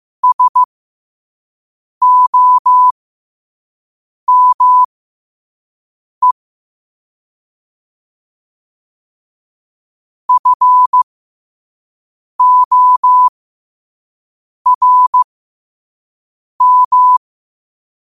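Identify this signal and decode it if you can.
Morse code, heard as a keyed tone, about 15 wpm, spaced as for 5 wpm, 1000 Hz. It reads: SOME FORM
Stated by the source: Morse code at 5 wpm.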